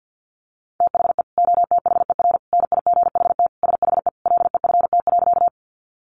Morse code, heard as a keyed tone, about 50 wpm, 700 Hz. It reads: THE OTHER NIGHT SHE LEFT.
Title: Sentences from Top 300 Words 50wpm